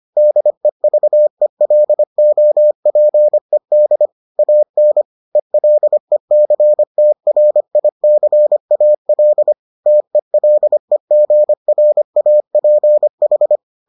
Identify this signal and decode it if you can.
Morse code, heard as a keyed tone, about 25 wpm, 600 Hz. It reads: DEVELOPED AN ELECTRICAL TELEGRAPH